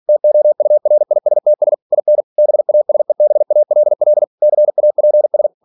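Morse code, heard as a keyed tone, about 46 wpm, 600 Hz. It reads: TOURISTS IN BASEBALL CAPS
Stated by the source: Practice text from an English novel sent at 47 WPM.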